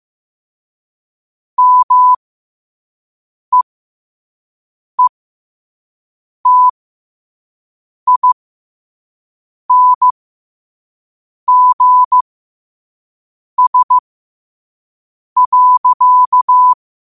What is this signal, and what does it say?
Morse code, heard as a keyed tone, about 15 wpm, spaced as for 5 wpm, 1000 Hz. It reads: MEETINGS.
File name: qotdfiles / 5wpm